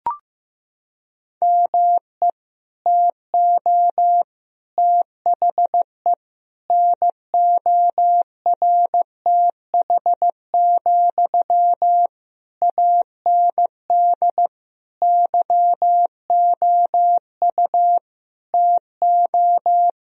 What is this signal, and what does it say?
Morse code, heard as a keyed tone, about 15 wpm, 700 Hz. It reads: ME TO THE NORTH, AND YOU TO
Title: Sentences from Top 1000 Words - Review Part 2 - 15wpm